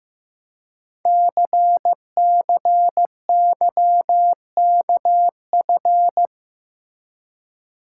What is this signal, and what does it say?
Morse code, heard as a keyed tone, about 15 wpm, 700 Hz. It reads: CCYKF
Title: Code Groups - 5 Letters - 15wpm